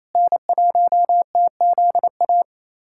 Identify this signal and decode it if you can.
Morse code, heard as a keyed tone, about 27 wpm, 690 Hz. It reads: N1TZA
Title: USA Callsign DE Callsign - 5 Characters - 28wpm